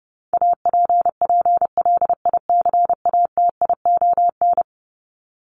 Morse code, heard as a keyed tone, about 30 wpm, 700 Hz. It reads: APPLICATION
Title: Top 1001-1250 Words 30wpm